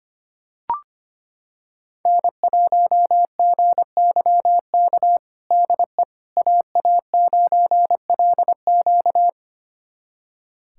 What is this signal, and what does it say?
Morse code, heard as a keyed tone, about 25 wpm, 700 Hz. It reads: N1GYK DE AA9LQ
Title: USA Callsign DE Callsign - 5 Characters - 25wpm